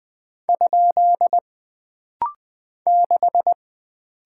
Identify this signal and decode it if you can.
Morse code, heard as a keyed tone, about 20 wpm, 700 Hz. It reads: ? 6
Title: Single Character (Letter:Number:Punctuation:Prosign) - Rapid-Fire - 20wpm